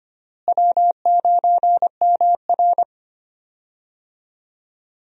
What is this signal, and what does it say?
Morse code, heard as a keyed tone, about 25 wpm, 700 Hz. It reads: W9MR
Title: USA Call Signs - 4 Characters - Repeated 3x - 25wpm